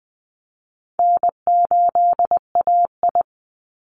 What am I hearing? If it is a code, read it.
Morse code, heard as a keyed tone, about 20 wpm, 700 Hz. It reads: N8AI
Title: Call Signs Repeated 3x - 20wpm